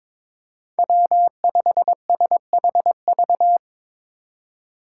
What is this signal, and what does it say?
Morse code, heard as a keyed tone, about 22 wpm, 700 Hz. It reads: W5SHV
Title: USA Call Signs - 5 Characters - 4x Character Spacing - 22wpm